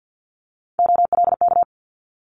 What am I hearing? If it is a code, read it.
Morse code, heard as a keyed tone, about 50 wpm, 700 Hz. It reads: OLK
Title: Multi-letter Phonemes Quick Fire 50wpm